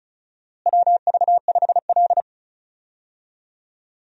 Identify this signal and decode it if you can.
Morse code, heard as a keyed tone, about 34 wpm, 700 Hz. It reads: WV5L
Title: USA Call Signs - 4 Characters - Repeated 2x - 35wpm